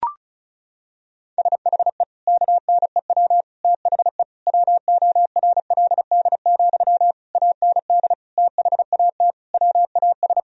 Morse code, heard as a keyed tone, about 35 wpm, 700 Hz. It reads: SHE KNEW THE WORLD, AND THAT WAS
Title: Sentences from Top 1000 Words - Review Part 4 - 35wpm